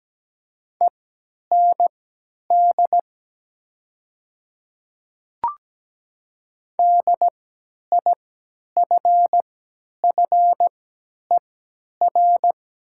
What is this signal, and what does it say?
Morse code, heard as a keyed tone, about 17 wpm, spaced as for 10 wpm, 700 Hz. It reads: END DIFFER
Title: 3x Character Spacing - Top 300 Words - 17wpm